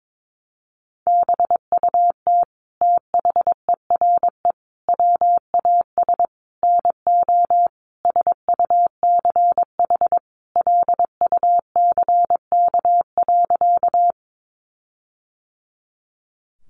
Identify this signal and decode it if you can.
Morse code, heard as a keyed tone, about 22 wpm, 700 Hz. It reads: BUT THERE WAS NO SUCH LUCK.
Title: Sentences from Top 1500 Words 22wpm